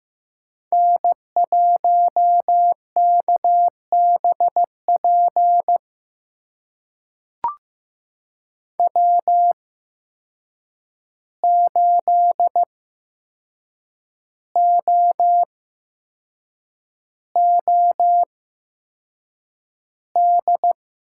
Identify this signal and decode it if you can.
Morse code, heard as a keyed tone, about 15 wpm, spaced as for 6 wpm, 700 Hz. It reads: N1KBP W8OOD
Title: USA Call Signs - 5 Characters - 8x Character Spacing - 15wpm